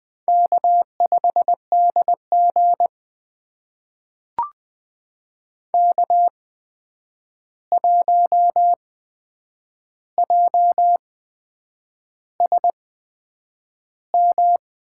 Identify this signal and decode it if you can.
Morse code, heard as a keyed tone, about 20 wpm, spaced as for 8 wpm, 700 Hz. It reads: K5DG K1JSM
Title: USA Call Signs - 8x Character Spacing - 20wpm